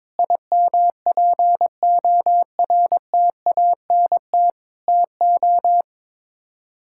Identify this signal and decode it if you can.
Morse code, heard as a keyed tone, about 22 wpm, 700 Hz. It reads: IMPORTANT TO
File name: practice / 22wpm / Build A Phrase - 5-Word Phrases - 22wpm